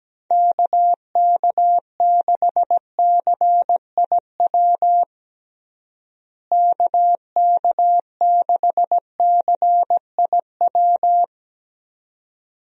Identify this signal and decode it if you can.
Morse code, heard as a keyed tone, about 17 wpm, 700 Hz. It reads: KK6CIW KK6CIW
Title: USA Call Signs - 6 Characters - Repeated 2x - 17wpm